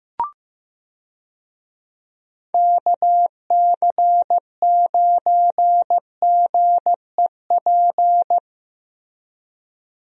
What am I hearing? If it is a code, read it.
Morse code, heard as a keyed tone, about 15 wpm, 700 Hz. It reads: KC9GEP
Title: USA Call Signs - 6 Characters - Repeated 2x - 15wpm